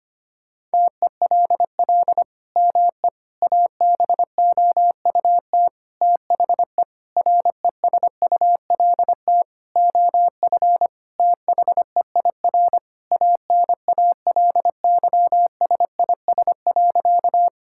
Morse code, heard as a keyed tone, about 25 wpm, 700 Hz. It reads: TELL ME ABOUT THE RESULT OF THEIR ANALYSIS.
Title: Sentences from Top 1000 Words - Review Part 5 - 25wpm